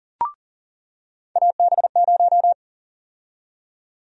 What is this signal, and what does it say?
Morse code, heard as a keyed tone, about 40 wpm, 700 Hz. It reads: AB0
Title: USA Call Sign Prefixes - 40wpm